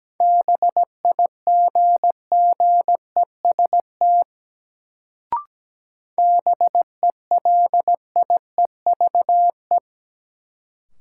Morse code, heard as a keyed tone, about 17 wpm, 700 Hz. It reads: BIGGEST BELIEVE
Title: Sets of 4 Words - New Word Drill - 17wpm